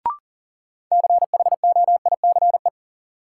Morse code, heard as a keyed tone, about 38 wpm, 700 Hz. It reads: CHOICE
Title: Sets of 5 Words - New Word Drill - 40wpm